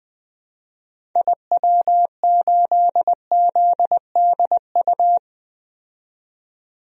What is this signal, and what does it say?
Morse code, heard as a keyed tone, about 20 wpm, 700 Hz. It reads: IW8ZDU
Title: DXCC-100 Call Signs - 6 Characters - 20wpm